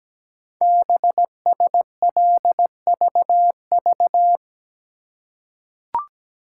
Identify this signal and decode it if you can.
Morse code, heard as a keyed tone, about 17 wpm, 700 Hz. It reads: BSLVV
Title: Code Groups - 5 Letters - 17wpm